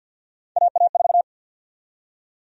Morse code, heard as a keyed tone, about 53 wpm, 700 Hz. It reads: AA4
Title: USA Call Sign Prefixes - 50wpm